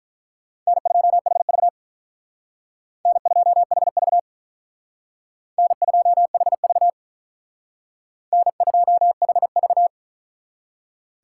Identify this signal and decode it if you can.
Morse code, heard as a keyed tone, about 43 wpm, 700 Hz. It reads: N2HV N2HV N2HV N2HV
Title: Speed Racing - USA 1x2 Call Signs 35wpm